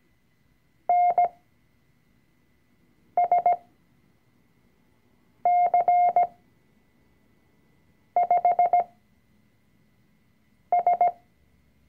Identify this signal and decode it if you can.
Morse code, heard as a keyed tone, about 17 wpm, spaced as for 4 wpm, 710 Hz. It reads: NSC5S
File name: CW-Online-Kurs / Anfänger / Lektion 004